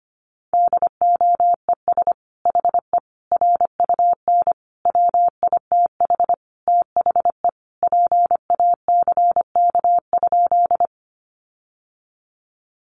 Morse code, heard as a keyed tone, about 25 wpm, 700 Hz. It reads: DOES HE RUN WITH THE PACK?
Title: Sentences from Top 1300 Words 25wpm